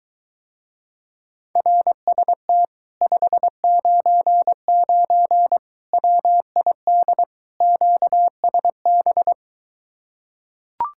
Morse code, heard as a keyed tone, about 23 wpm, 700 Hz. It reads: RST 599 WID QSB